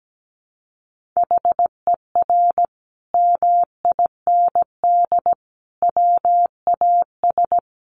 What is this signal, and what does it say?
Morse code, heard as a keyed tone, about 17 wpm, 700 Hz. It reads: HER MIND WAS
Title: Sentences from Top 1500 Words 17wpm